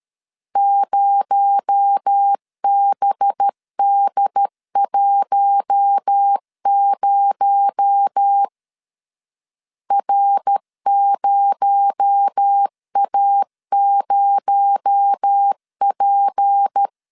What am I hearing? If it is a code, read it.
Morse code, heard as a keyed tone, about 13 wpm, 800 Hz. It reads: 0BD10 R0A0P
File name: LEZIONI / 6 PKX0 / 6 PKX0 12wpm 2231